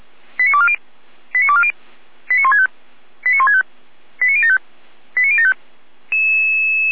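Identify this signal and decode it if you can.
Five-tone selective call (ZVEI), heard as quick runs of five tones, tones of about 70 ms. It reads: 89240 89240 89165 89165 89075 89075
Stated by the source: Five-tone ZVEI selective call received on SDR, recorded by feuerwehr-orlamuende.org fire brigade.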